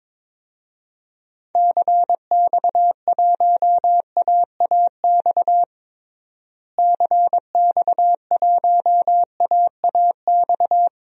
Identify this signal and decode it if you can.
Morse code, heard as a keyed tone, about 22 wpm, 700 Hz. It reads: CX1AAX CX1AAX
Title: DXCC-100 Call Signs - 6 Characters - Repeated 3x - 22wpm